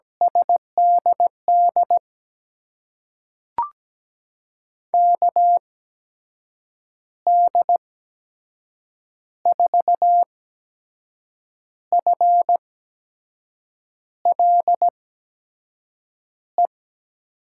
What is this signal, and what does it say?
Morse code, heard as a keyed tone, about 17 wpm, spaced as for 6 wpm, 700 Hz. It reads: SDD KD4FLE
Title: USA Call Signs - 6 Characters - 8x Character Spacing - 17wpm